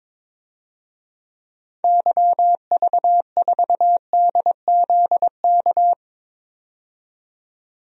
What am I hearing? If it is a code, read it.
Morse code, heard as a keyed tone, about 22 wpm, 700 Hz. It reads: YV4DZK
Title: DXCC-100 Call Signs - 6 Characters - 22wpm